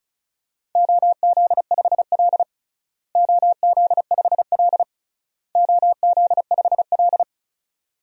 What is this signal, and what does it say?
Morse code, heard as a keyed tone, about 34 wpm, 700 Hz. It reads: OZ5L OZ5L OZ5L
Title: DXCC-100 Call Signs - 4 Characters - Repeated 3x - 35wpm